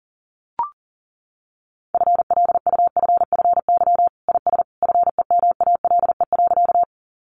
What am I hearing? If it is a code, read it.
Morse code, heard as a keyed tone, about 40 wpm, 700 Hz. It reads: FLUFFY IS FEMALE.
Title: Easy Phrases 40wpm